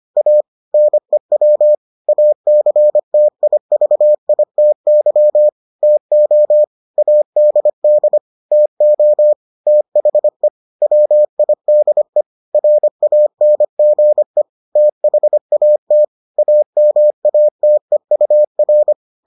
Morse code, heard as a keyed tone, about 25 wpm, 600 Hz. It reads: A NEW ACTIVITY TO ADD TO THE WIDE RANGE THAT AMATEUR